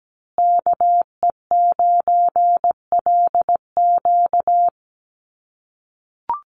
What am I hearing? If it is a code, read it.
Morse code, heard as a keyed tone, about 17 wpm, 700 Hz. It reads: KE9LQ
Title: USA Call Signs Repeated 3x 17wpm